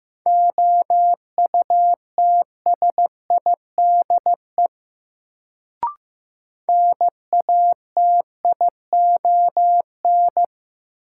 Morse code, heard as a keyed tone, about 15 wpm, 700 Hz. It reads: OUTSIDE NATION
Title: Sentences from Top 900 Words - New Word Drill - 15wpm